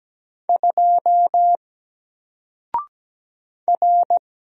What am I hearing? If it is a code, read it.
Morse code, heard as a keyed tone, about 17 wpm, 700 Hz. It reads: 2 R
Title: Single Letter-Number 17wpm